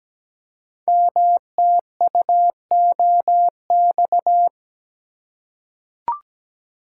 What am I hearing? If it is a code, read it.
Morse code, heard as a keyed tone, about 17 wpm, 700 Hz. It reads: MTUOX
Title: Code Groups - 5 Letters - 17wpm